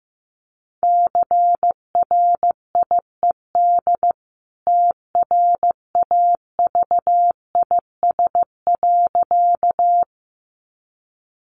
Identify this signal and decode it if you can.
Morse code, heard as a keyed tone, about 15 wpm, 700 Hz. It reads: CRIED TRAVIS.
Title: A Sound Of Thunder 15wpm